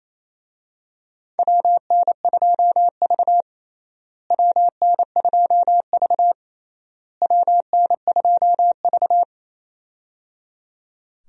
Morse code, heard as a keyed tone, about 28 wpm, 700 Hz. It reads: WN2V WN2V WN2V
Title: USA Call Signs - 4 Characters - Repeated 3x - 28wpm